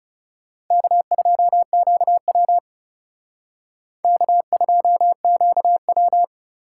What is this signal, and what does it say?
Morse code, heard as a keyed tone, about 37 wpm, 700 Hz. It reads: K2QW K2QW
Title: Speed Racing - USA 1x2 Call Signs 30wpm